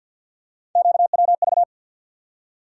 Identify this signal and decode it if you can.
Morse code, heard as a keyed tone, about 51 wpm, 700 Hz. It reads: QWV
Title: USA Call Sign Postfixes - 3 Letters 50wpm